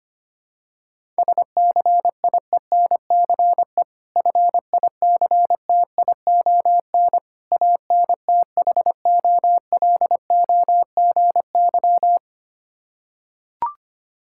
Morse code, heard as a keyed tone, about 25 wpm, 700 Hz. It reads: SCIENCE FICTION ANTHOLOGY
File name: practice / 25wpm / Difficult Words with Hints - 25wpm